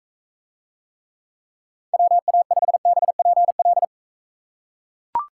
Morse code, heard as a keyed tone, about 42 wpm, 700 Hz. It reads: WA5BPL